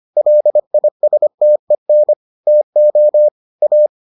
Morse code, heard as a keyed tone, about 25 wpm, 600 Hz. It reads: LISTEN TO A